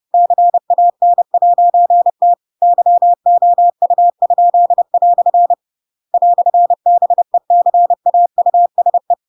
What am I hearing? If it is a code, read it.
Morse code, heard as a keyed tone, about 30 wpm, 700 Hz. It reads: CAN'T YOU?" "BECAUSE
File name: Pinocchio / Pinocchio ch7 30 30 ik1hge